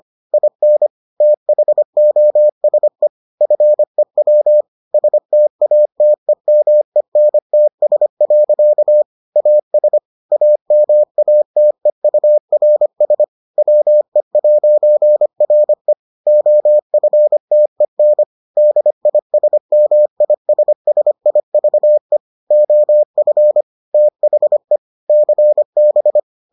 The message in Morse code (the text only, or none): IN THOSE FEW STATEMENTS. AS AMATEURS WE'RE OFTEN DISMISSIVE OF THE CB